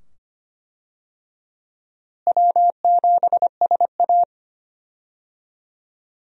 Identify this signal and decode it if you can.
Morse code, heard as a keyed tone, about 25 wpm, 700 Hz. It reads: W7SA
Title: USA Call Signs - 4 Characters - Repeated 2x - 25wpm